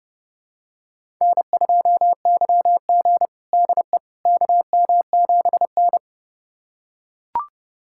Morse code, heard as a keyed tone, about 30 wpm, 700 Hz. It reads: N2YG DE KM7N
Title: USA Callsign DE Callsign - 4 Characters - 30wpm